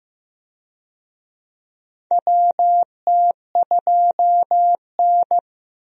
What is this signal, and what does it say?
Morse code, heard as a keyed tone, about 15 wpm, 700 Hz. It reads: WT2N
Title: USA Call Signs - 4 Characters - Repeated 2x - 15wpm